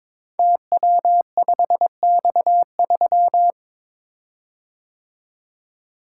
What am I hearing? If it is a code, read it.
Morse code, heard as a keyed tone, about 22 wpm, 690 Hz. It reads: TW5X3
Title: Code Groups - 5 Letter:Numbers - 22wpm